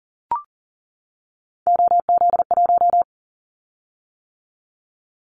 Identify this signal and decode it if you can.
Morse code, heard as a keyed tone, about 40 wpm, 700 Hz. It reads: OZ1
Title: Callsign Prefixes for Top 100 DXCC - 40wpm